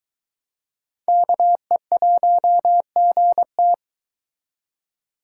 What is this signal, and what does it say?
Morse code, heard as a keyed tone, about 22 wpm, 700 Hz. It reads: KE1GT